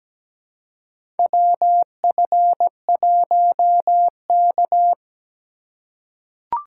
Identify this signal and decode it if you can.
Morse code, heard as a keyed tone, about 17 wpm, 700 Hz. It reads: WF1K
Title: USA Call Signs - 4x Character Spacing - 17wpm